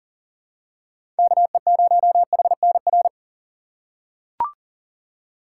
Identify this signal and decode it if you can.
Morse code, heard as a keyed tone, about 37 wpm, 700 Hz. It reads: KE0HNR
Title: USA Call Signs - 3x Character Spacing - 40wpm